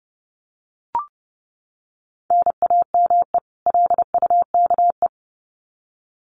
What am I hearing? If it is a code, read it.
Morse code, heard as a keyed tone, about 30 wpm, 700 Hz. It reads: NAME LUKE
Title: Common QSO Phrases 30wpm